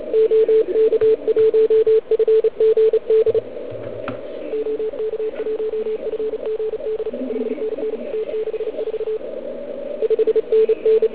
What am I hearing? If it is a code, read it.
Morse code, heard as a keyed tone, about 34 wpm, 440 Hz, with radio noise. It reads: OK1FGD OK1FGD 5NN T64 5NN